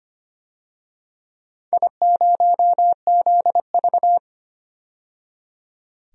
Morse code, heard as a keyed tone, about 25 wpm, 700 Hz. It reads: I0ZV
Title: DXCC-100 Call Signs - 4 Characters - 25wpm